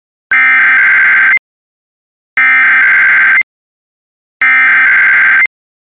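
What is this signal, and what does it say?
Emergency Alert System (SAME) tones (header bursts), heard as harsh buzzing bursts. It is heard from 0.3 to 5.4 s.
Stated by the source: Emergency alert tones from a Weatherscan IntelliStar weather channel system.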